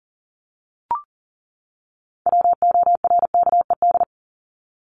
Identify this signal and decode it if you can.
Morse code, heard as a keyed tone, about 40 wpm, 700 Hz. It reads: WORKED
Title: ED Suffix - Top 100 Words with Base Words 40wpm